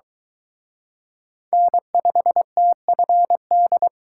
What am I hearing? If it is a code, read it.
Morse code, heard as a keyed tone, about 22 wpm, 700 Hz. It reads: N5TFD